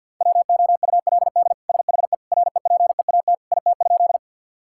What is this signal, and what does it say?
Morse code, heard as a keyed tone, about 51 wpm, 700 Hz. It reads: WOULD SHE REPEAT IT?
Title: Sentences from Top 1600 Words 50wpm